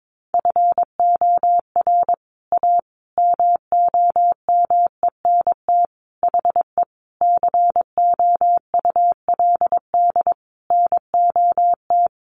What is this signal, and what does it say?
Morse code, heard as a keyed tone, about 22 wpm, 700 Hz. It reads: FOR A MOMENT HE COULD NOT
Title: Sentences from Top 1100 Words 22wpm